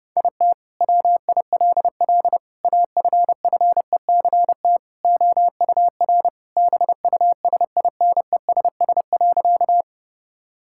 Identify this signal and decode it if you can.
Morse code, heard as a keyed tone, about 32 wpm, 700 Hz. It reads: IT WILL AFFECT OUR BUSINESS.
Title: Sentences from Top 1000 Words - Review Part 6 - 30wpm